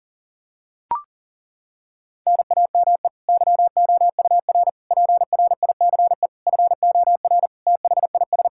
Morse code, heard as a keyed tone, about 40 wpm, 700 Hz. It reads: NAME YOUR PRICE FOR THIS
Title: Sentences from Top 1000 Words - Review Part 5 - 40wpm